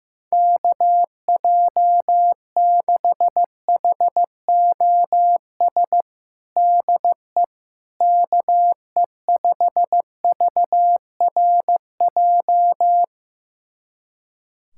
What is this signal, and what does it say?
Morse code, heard as a keyed tone, about 15 wpm, 700 Hz. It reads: KJ6HOS DE KE5VRJ
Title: USA Callsign DE Callsign - 6 Characters - 15wpm